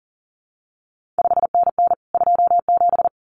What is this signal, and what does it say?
Morse code, heard as a keyed tone, about 38 wpm, 700 Hz. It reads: 5NN 27
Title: CQ WW DX Contest Exchange Encore - 40wpm